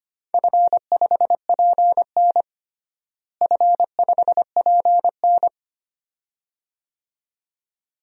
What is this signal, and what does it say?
Morse code, heard as a keyed tone, about 25 wpm, 700 Hz. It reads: F5PN F5PN
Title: DXCC-100 Call Signs - 4 Characters - Repeated 2x - 25wpm